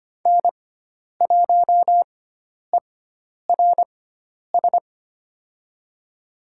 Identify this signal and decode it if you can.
Morse code, heard as a keyed tone, about 25 wpm, spaced as for 10 wpm, 700 Hz. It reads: N1ERS
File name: Lesson 16g - Letter c - Call Signs 25wpm